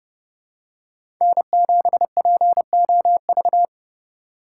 Morse code, heard as a keyed tone, about 27 wpm, 700 Hz. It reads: N7POV